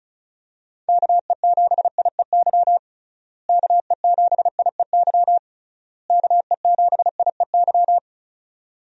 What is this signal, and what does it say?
Morse code, heard as a keyed tone, about 36 wpm, 700 Hz. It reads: KE7IEY KE7IEY KE7IEY
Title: USA Call Signs - 6 Characters - Repeated 3x - 35wpm